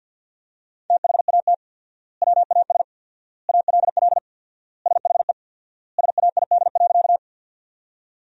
Morse code, heard as a keyed tone, about 50 wpm, 700 Hz. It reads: THAT WAS ALL SHE SAID.
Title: 4x Word-Spacing - Sentences from Top 100 Words - 50wpm